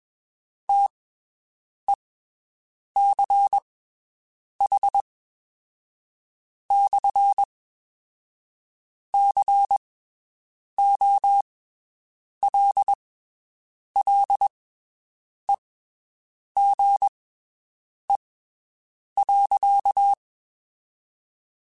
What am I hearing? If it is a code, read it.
Morse code, heard as a keyed tone, about 21 wpm, spaced as for 7 wpm, 800 Hz. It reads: TECH/COLLEGE.
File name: (spaced) / QSOs Spaced-mp3s / WD0BGP 1 @20 sp